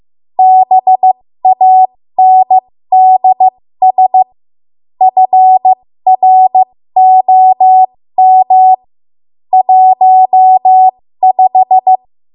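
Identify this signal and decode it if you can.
Morse code, heard as a keyed tone, about 15 wpm, spaced as for 13 wpm, 750 Hz. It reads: BANDS FROM 15
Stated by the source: 13 wpm Morse code practice archive transmission.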